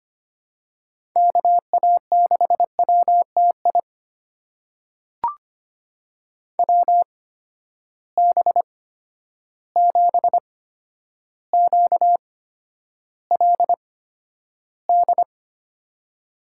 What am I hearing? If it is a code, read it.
Morse code, heard as a keyed tone, about 25 wpm, spaced as for 11 wpm, 700 Hz. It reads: KA6WTI WB7QLD
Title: USA Call Signs - 6 Characters - 8x Character Spacing - 25wpm